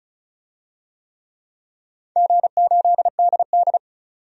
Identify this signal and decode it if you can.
Morse code, heard as a keyed tone, about 36 wpm, 700 Hz. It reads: G8DD
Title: DXCC-100 Call Signs - 4 Characters - 35wpm